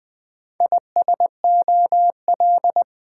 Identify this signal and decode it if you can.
Morse code, heard as a keyed tone, about 21 wpm, 700 Hz. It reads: ISOL
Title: Build A Word - 2 Letters At A Time - 8-Letter Words - 20wpm